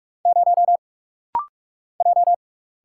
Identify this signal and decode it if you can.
Morse code, heard as a keyed tone, about 45 wpm, 700 Hz. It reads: 0 J
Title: Single Letter-Number - Rapid-Fire 45wpm